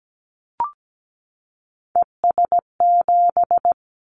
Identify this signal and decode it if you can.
Morse code, heard as a keyed tone, about 17 wpm, 700 Hz. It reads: ES7
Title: Callsign Prefixes for Top 100 DXCC - 17wpm